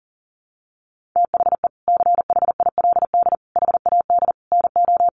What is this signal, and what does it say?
Morse code, heard as a keyed tone, about 40 wpm, 700 Hz. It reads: THE CHILD HAD NO
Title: Sentences from Top 1400 Words 40wpm